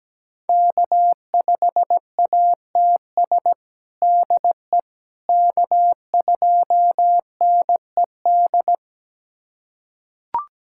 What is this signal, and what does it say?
Morse code, heard as a keyed tone, about 17 wpm, 700 Hz. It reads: K5ATS DE K2NED
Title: USA Callsign DE Callsign - 5 Characters - 17wpm